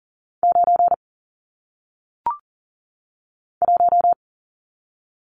Morse code, heard as a keyed tone, about 40 wpm, 700 Hz. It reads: 9 1
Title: Numbers - 1-Digit 40wpm